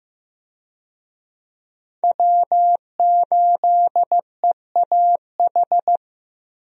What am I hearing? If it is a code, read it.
Morse code, heard as a keyed tone, about 15 wpm, 700 Hz. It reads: W8EAH
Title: USA Call Signs - 5 Characters - Repeated 2x - 15wpm